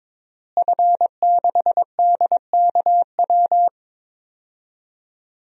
Morse code, heard as a keyed tone, about 22 wpm, 700 Hz. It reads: F6DKW
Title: DXCC-100 Call Signs - 5 Characters - Repeated 3x - 22wpm